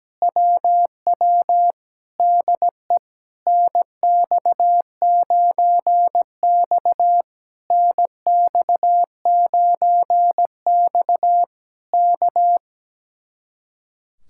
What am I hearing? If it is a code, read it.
Morse code, heard as a keyed tone, about 17 wpm, 700 Hz. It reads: WW DE NX9X NX9X K
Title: Speed-Racing Common QSO Phrases Encore - 17wpm